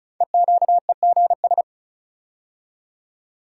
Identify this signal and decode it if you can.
Morse code, heard as a keyed tone, about 36 wpm, 690 Hz. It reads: EQEGS
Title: Code Groups - 5 Letters - 35wpm